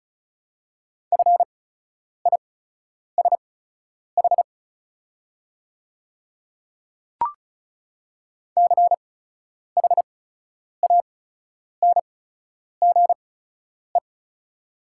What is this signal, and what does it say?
Morse code, heard as a keyed tone, about 36 wpm, spaced as for 10 wpm, 700 Hz. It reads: FISH CHANGE
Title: 8x Character Spacing - Top 500 Words - 35wpm